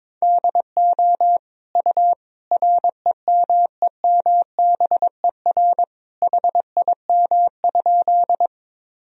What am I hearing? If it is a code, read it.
Morse code, heard as a keyed tone, about 22 wpm, 700 Hz. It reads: DO U REMEMBER HIM?
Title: Sentences from Top 400 Words and Abbreviations 22wpm